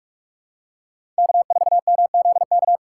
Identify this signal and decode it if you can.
Morse code, heard as a keyed tone, about 46 wpm, 700 Hz. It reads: K4MZK
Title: USA Call Signs - 5 Characters - 2x Character Spacing - 45wpm